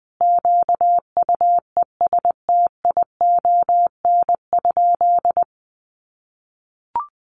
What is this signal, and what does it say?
Morse code, heard as a keyed tone, about 20 wpm, 700 Hz. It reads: QUESTION?
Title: Sentences from Top 400 Words 20wpm